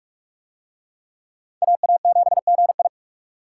Morse code, heard as a keyed tone, about 45 wpm, 700 Hz. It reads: AA7GI